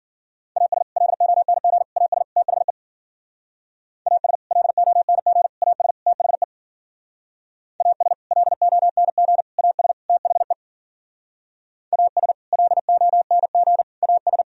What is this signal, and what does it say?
Morse code, heard as a keyed tone, about 52 wpm, 700 Hz. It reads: AS LONG AS THE AS LONG AS THE AS LONG AS THE AS LONG AS